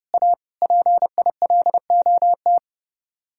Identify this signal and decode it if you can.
Morse code, heard as a keyed tone, about 35 wpm, 700 Hz. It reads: A PILOT
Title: Speed-Racing Common QSO Phrases Encore - 30wpm